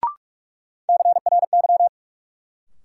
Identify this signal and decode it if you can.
Morse code, heard as a keyed tone, about 46 wpm, 700 Hz. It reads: CRY
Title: Sentences from Top 500 Words - New Word Drill - 45wpm